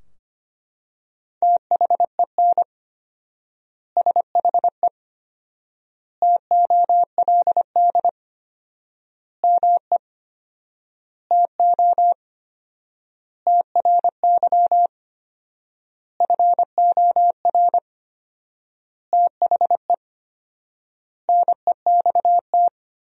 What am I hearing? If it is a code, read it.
Morse code, heard as a keyed tone, about 25 wpm, 700 Hz. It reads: THEN SHE TOLD ME TO TRY FOR THE NEXT
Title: 4x Word-Spacing - Sentences from Top 400 Words - 25wpm